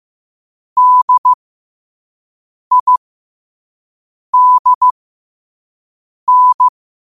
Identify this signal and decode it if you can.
Morse code, heard as a keyed tone, about 15 wpm, spaced as for 5 wpm, 1000 Hz. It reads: DIDN
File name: qotdfiles / 5wpm